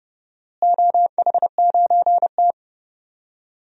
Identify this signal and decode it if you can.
Morse code, heard as a keyed tone, about 32 wpm, 700 Hz. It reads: OH9T
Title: DXCC-100 Call Signs - 4 Characters - Repeated 2x - 30wpm